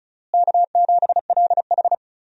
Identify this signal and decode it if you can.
Morse code, heard as a keyed tone, about 34 wpm, 700 Hz. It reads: K7LH
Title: USA Call Signs - 4 Characters - Repeated 2x - 35wpm